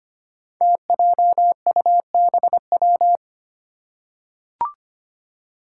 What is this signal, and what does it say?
Morse code, heard as a keyed tone, about 25 wpm, 700 Hz. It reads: TJUBW
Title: Five Challenging Letter-Number Combinations 25wpm